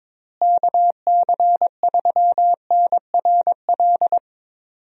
Morse code, heard as a keyed tone, about 22 wpm, 700 Hz. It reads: KC3NRL
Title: USA Call Signs - 6 Characters - Repeated 3x - 22wpm